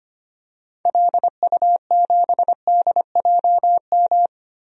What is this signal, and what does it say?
Morse code, heard as a keyed tone, about 25 wpm, 700 Hz. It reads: LU7DJM
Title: DXCC-100 Call Signs - 6 Characters - Repeated 2x - 25wpm